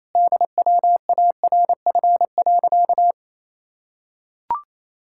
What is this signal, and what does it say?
Morse code, heard as a keyed tone, about 28 wpm, 700 Hz. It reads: DWARF.
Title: Pangram Challenge v2 - 28wpm